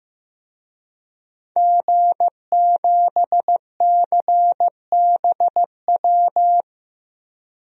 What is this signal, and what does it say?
Morse code, heard as a keyed tone, about 15 wpm, 700 Hz. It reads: G7CBW
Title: DXCC-100 Call Signs - 5 Characters - 15wpm